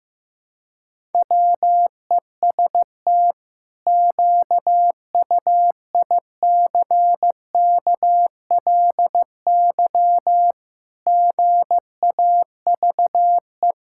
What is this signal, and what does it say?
Morse code, heard as a keyed tone, about 15 wpm, 700 Hz. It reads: WEST QUICKLY GAVE
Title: Pangram Challenge v2 - 15wpm